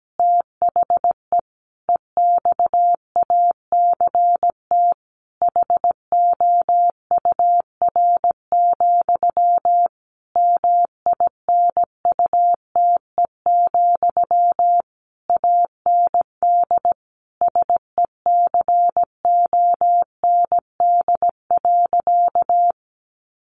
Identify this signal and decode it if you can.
Morse code, heard as a keyed tone, about 17 wpm, 700 Hz. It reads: THE EXACT HOUR, MINUTE, AND SECOND.
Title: A Sound Of Thunder 17wpm